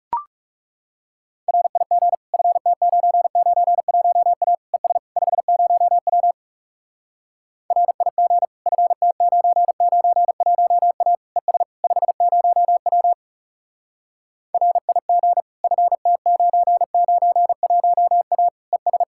Morse code, heard as a keyed tone, about 40 wpm, 700 Hz. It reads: RIG FT991A ES 50W RIG FT991A ES 50W RIG FT991A ES